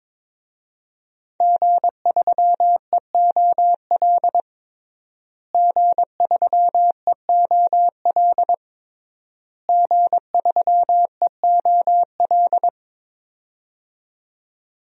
Morse code, heard as a keyed tone, about 22 wpm, 700 Hz. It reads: G3EOL G3EOL G3EOL
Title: DXCC-100 Call Signs - 5 Characters - Repeated 3x - 22wpm